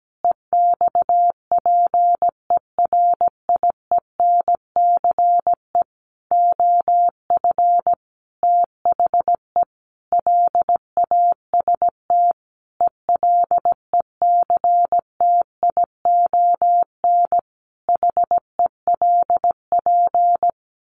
Morse code, heard as a keyed tone, about 17 wpm, 700 Hz. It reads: EXPERIENCE OF THE LAST ELECTION HELP
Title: Sentences from Top 1500 Words 17wpm